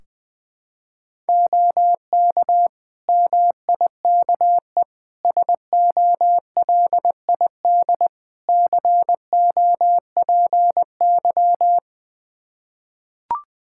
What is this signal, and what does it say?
Morse code, heard as a keyed tone, about 20 wpm, 700 Hz. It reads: OK MIKE SOLID COPY
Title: Practice QSO - Exchange 3 - More Info - 20wpm